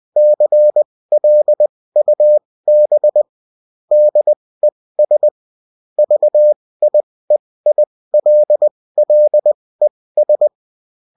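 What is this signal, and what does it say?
Morse code, heard as a keyed tone, about 20 wpm, spaced as for 16 wpm, 600 Hz. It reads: CLUB DES VIEILLES
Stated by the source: UFT bulletin sent in CW.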